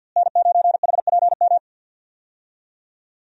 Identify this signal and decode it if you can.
Morse code, heard as a keyed tone, about 47 wpm, 700 Hz. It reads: N9HPM
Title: USA Call Signs - 5 Characters - Repeated 3x - 50wpm